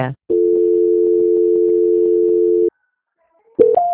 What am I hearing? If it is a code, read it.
Dial tone (North America), 350 + 440 Hz.